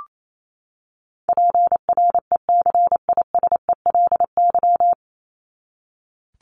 Morse code, heard as a keyed tone, about 28 wpm, 700 Hz. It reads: PRECISELY
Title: LY Suffix - Top 100 Words with Base Words 28wpm